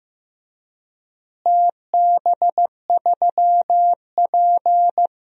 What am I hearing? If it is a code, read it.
Morse code, heard as a keyed tone, about 15 wpm, 700 Hz. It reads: TB3P
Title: DXCC-100 Call Signs - 4 Characters - Repeated 3x - 15wpm